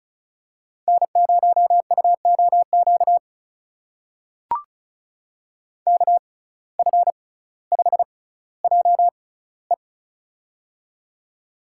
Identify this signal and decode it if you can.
Morse code, heard as a keyed tone, about 36 wpm, spaced as for 18 wpm, 700 Hz. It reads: N0UOQ KF5JE
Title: USA Call Signs - 6x Character Spacing - 35wpm